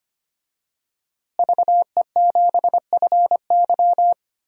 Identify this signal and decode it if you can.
Morse code, heard as a keyed tone, about 25 wpm, 700 Hz. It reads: VE7FY
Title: DXCC-100 Call Signs - 5 Characters - Repeated 3x - 25wpm